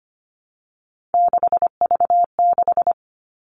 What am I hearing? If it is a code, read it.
Morse code, heard as a keyed tone, about 25 wpm, 700 Hz. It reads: 6V6
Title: Three Challenging Letter-Number Combinations 25wpm